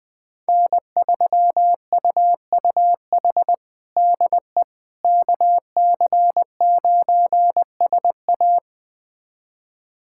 Morse code, heard as a keyed tone, about 20 wpm, 700 Hz. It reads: N3UUH DE KC9SA
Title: USA Callsign DE Callsign - 5 Characters - 20wpm